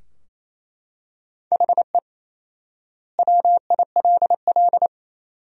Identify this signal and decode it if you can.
Morse code, heard as a keyed tone, about 28 wpm, 700 Hz. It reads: HE WILL
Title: 4x Word-Spacing - Sentences from Top 400 Words - 28wpm